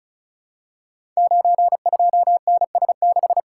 Morse code, heard as a keyed tone, about 36 wpm, 700 Hz. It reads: 92NS6
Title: Code Groups - 5 Letter:Numbers - 35wpm